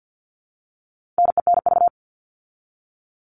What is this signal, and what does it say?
Morse code, heard as a keyed tone, about 50 wpm, 700 Hz. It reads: NENV
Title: All Cut Numbers - 4-Digit 50wpm